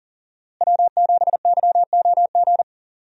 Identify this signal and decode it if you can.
Morse code, heard as a keyed tone, about 40 wpm, 700 Hz. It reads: W7YOG
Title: USA Call Signs - 5 Characters - Repeated 3x - 40wpm